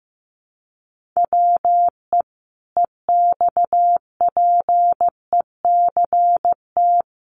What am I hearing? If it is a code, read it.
Morse code, heard as a keyed tone, about 15 wpm, 700 Hz. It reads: WE EXPECT
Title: Sentences from Top 1100 Words 15wpm